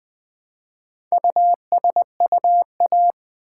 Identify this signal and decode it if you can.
Morse code, heard as a keyed tone, about 20 wpm, 700 Hz. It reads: USUA
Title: Build A Word - 1 Letter At A Time - 5-Letter Words - 20wpm